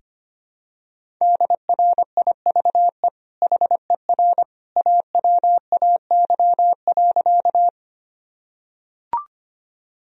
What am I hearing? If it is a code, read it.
Morse code, heard as a keyed tone, about 25 wpm, 700 Hz. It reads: DRIVE HER AWAY.
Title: Sentences from Top 1000 Words - Review Part 2 - 25wpm